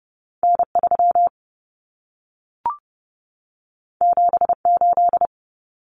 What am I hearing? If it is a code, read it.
Morse code, heard as a keyed tone, about 30 wpm, 700 Hz. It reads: N3 78
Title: Common Cut Numbers - 2-Digit 30wpm